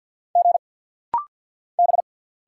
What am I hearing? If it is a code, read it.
Morse code, heard as a keyed tone, about 50 wpm, 700 Hz. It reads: G B
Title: Single Letter-Number Alt - Rapid-Fire 50wpm